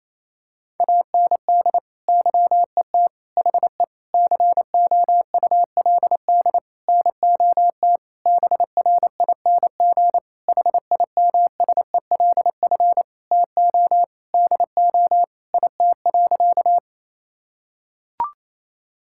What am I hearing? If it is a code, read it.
Morse code, heard as a keyed tone, about 28 wpm, 700 Hz. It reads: AND YET HE COULD NOT BRING HIMSELF TO DO IT.
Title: Sentences from Top 1600 Words 28wpm